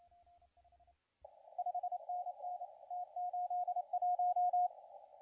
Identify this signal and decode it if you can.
Morse code, heard as a keyed tone, about 27 wpm, 750 Hz, with radio noise. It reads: 5NN T81
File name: la4c / sac-cw-2017 / CONTEST AUDIO 2100Z